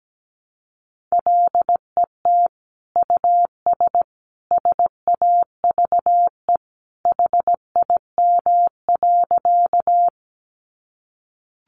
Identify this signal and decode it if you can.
Morse code, heard as a keyed tone, about 17 wpm, 700 Hz. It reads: LET US SAVE HIM.
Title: Sentences from Top 1100 Words 17wpm